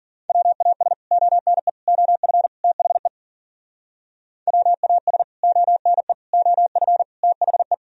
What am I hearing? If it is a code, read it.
Morse code, heard as a keyed tone, about 52 wpm, 700 Hz. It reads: WAS ONE OF THE WAS ONE OF THE